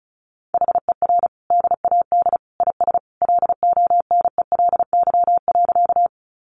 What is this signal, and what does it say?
Morse code, heard as a keyed tone, about 34 wpm, 700 Hz. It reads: HER DAD IS LONELY.
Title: Additional Easy Phrases 35wpm